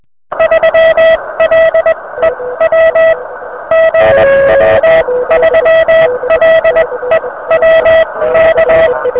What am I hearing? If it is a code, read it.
Morse code, heard as a keyed tone, about 21 wpm, spaced as for 17 wpm, 650 Hz, with noise. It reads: 3LEW GW3LEW K
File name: VP9-N0JK-pile